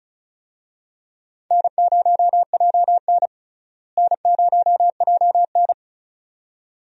Morse code, heard as a keyed tone, about 36 wpm, 700 Hz. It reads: N0JN N0JN
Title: USA Call Signs - 4 Characters - Repeated 2x - 35wpm